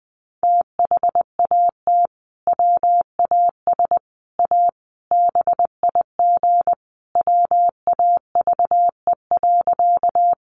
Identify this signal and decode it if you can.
Morse code, heard as a keyed tone, about 20 wpm, 700 Hz. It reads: THAT WAS A BIG WAVE.
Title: Sentences from Top 1400 Words 20wpm